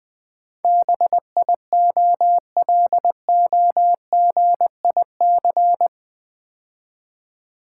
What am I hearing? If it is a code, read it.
Morse code, heard as a keyed tone, about 20 wpm, 700 Hz. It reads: BIOLOGIC
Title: Build A Word - 4 Letters At A Time - 10-Letter Words - 20wpm